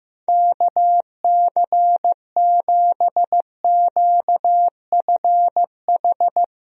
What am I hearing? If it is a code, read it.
Morse code, heard as a keyed tone, about 15 wpm, 700 Hz. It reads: KC7QFH
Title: USA Call Signs - 6 Characters - Repeated 3x - 15wpm